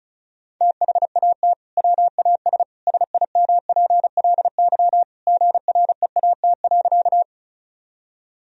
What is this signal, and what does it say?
Morse code, heard as a keyed tone, about 36 wpm, 700 Hz. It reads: THAT WAS SIMPLY GREAT.
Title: Sentences from Top 1000 Words - Review Part 4 - 35wpm